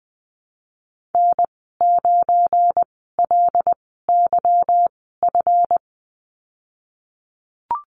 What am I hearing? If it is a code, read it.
Morse code, heard as a keyed tone, about 20 wpm, spaced as for 15 wpm, 700 Hz. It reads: N9LYF
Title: 186 Lesson 30g - Letter b - Call Signs 20wpm